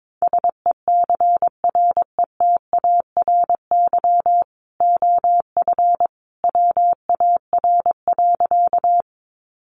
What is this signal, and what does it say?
Morse code, heard as a keyed tone, about 22 wpm, 700 Hz. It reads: SECRETARY OF WAR.
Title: Sentences from Top 1300 Words 22wpm